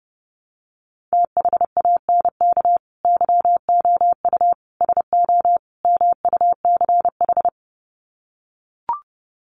Morse code, heard as a keyed tone, about 30 wpm, 700 Hz. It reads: THANK YOU SO MUCH
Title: Sets of 4 Words 30wpm